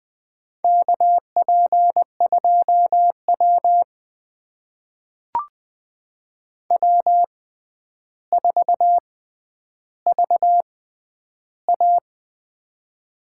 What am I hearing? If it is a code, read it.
Morse code, heard as a keyed tone, about 20 wpm, spaced as for 10 wpm, 700 Hz. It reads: KP2W W4VA
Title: USA Call Signs - 4 Characters - 6x Character Spacing - 20wpm